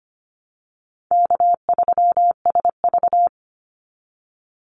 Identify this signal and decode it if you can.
Morse code, heard as a keyed tone, about 25 wpm, 700 Hz. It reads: K3SV
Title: Easy Call Signs Repeated 2x - 25wpm